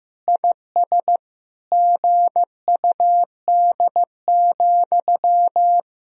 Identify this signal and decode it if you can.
Morse code, heard as a keyed tone, about 14 wpm, 700 Hz. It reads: IS GUD,
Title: Sentences from Top 200 Words and Abbreviations 15wpm